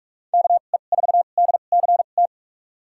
Morse code, heard as a keyed tone, about 42 wpm, spaced as for 31 wpm, 700 Hz. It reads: KE4DCT
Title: USA Call Signs - 6 Characters - 2x Character Spacing - 45wpm